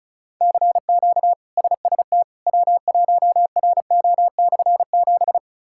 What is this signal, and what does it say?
Morse code, heard as a keyed tone, about 34 wpm, 700 Hz. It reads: CQ SST W1RO/7
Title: K1USN SST QSO Practice - 35wpm